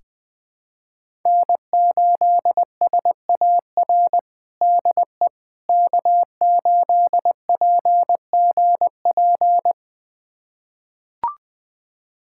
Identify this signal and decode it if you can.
Morse code, heard as a keyed tone, about 20 wpm, 700 Hz. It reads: N8SAR DE K8PGP
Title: USA Callsign DE Callsign - 5 Characters - 20wpm